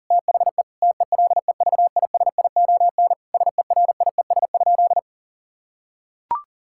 Morse code, heard as a keyed tone, about 40 wpm, 700 Hz. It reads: THE TELEVISION SERIES?
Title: Sentences from Top 1000 Words - Review Part 5 - 40wpm